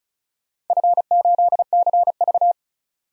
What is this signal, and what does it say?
Morse code, heard as a keyed tone, about 36 wpm, 700 Hz. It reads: F8CV
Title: DXCC-100 Call Signs - 4 Characters - 35wpm